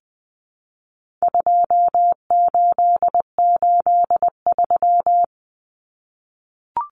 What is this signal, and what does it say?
Morse code, heard as a keyed tone, about 20 wpm, 700 Hz. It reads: 2883
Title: Common Cut Numbers - 4-Digit 20wpm